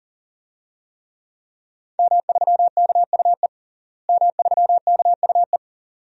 Morse code, heard as a keyed tone, about 40 wpm, 700 Hz. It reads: M3KUE M3KUE
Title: DXCC-100 Call Signs - 5 Characters - Repeated 2x - 40wpm